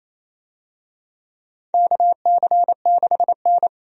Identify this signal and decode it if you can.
Morse code, heard as a keyed tone, about 28 wpm, 690 Hz. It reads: KC6N
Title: USA Call Signs - 4 Characters - 28wpm